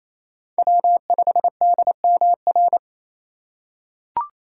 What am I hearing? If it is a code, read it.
Morse code, heard as a keyed tone, about 28 wpm, 700 Hz. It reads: W5DMR
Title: USA Call Signs - 4x Character Spacing - 28wpm